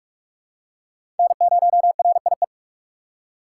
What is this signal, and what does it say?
Morse code, heard as a keyed tone, about 44 wpm, 700 Hz. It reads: N0RIE
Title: USA Call Signs - 5 Characters - 6x Character Spacing - 45wpm